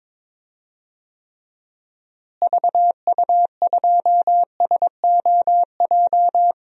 Morse code, heard as a keyed tone, about 22 wpm, 700 Hz. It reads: VU2SOJ
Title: DXCC-100 Call Signs - 6 Characters - Repeated 2x - 22wpm